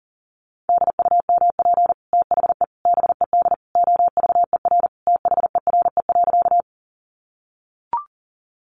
Morse code, heard as a keyed tone, about 40 wpm, 700 Hz. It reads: DUMP THE BED OVER THERE.
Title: Sentences from Top 1300 Words 40wpm